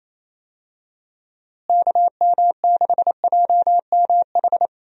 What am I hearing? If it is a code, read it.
Morse code, heard as a keyed tone, about 28 wpm, 700 Hz. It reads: KM6JMH
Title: USA Call Signs - 6 Characters - Repeated 3x - 28wpm